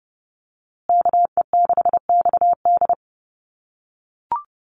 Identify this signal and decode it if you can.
Morse code, heard as a keyed tone, about 30 wpm, 700 Hz. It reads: KE6XD
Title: USA Call Signs Repeated 3x 30wpm